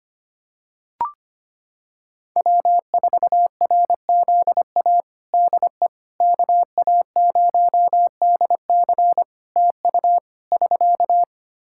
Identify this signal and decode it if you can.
Morse code, heard as a keyed tone, about 25 wpm, 700 Hz. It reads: W4RZA DE KA0DC TU <SK>
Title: Practice QSO - Exchange 4 - Ending - 25wpm